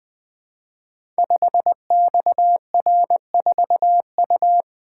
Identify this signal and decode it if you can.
Morse code, heard as a keyed tone, about 20 wpm, 700 Hz. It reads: 5XR4U
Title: Code Groups - 5 Letter:Numbers - 20wpm